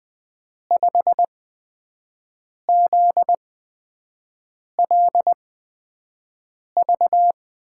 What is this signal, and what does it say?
Morse code, heard as a keyed tone, about 20 wpm, spaced as for 5 wpm, 700 Hz. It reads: 5ZLV
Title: USA Call Signs - 6 Characters - 8x Character Spacing - 20wpm